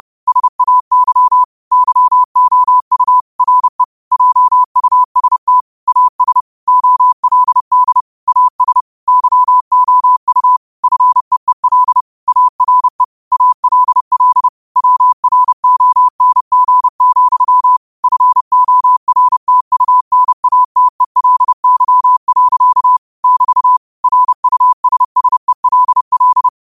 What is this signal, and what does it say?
Morse code, heard as a keyed tone, about 30 wpm, 1000 Hz. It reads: SAY YOU RE JUST AS OLD AS YOU FEEL ARE ALL WRONG, FORTUNATELY. <BT> RUSSELL
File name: qotdfiles / 30wpm